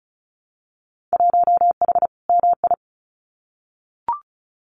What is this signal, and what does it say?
Morse code, heard as a keyed tone, about 36 wpm, 700 Hz. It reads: 1H MI
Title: Winter Field Day Exchanges 35wpm